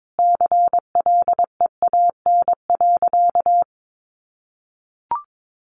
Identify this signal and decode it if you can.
Morse code, heard as a keyed tone, about 22 wpm, 700 Hz. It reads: CLEAN.
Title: Sentences from Top 1100 Words 22wpm